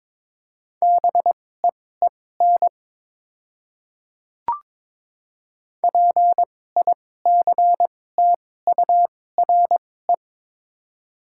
Morse code, heard as a keyed tone, about 22 wpm, spaced as for 16 wpm, 700 Hz. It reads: BEEN PICTURE
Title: 2x Character Spacing - Top 500 Words - 22wpm